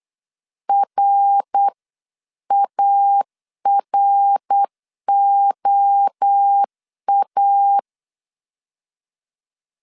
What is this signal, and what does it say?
Morse code, heard as a keyed tone, about 8 wpm, 800 Hz. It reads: RAROA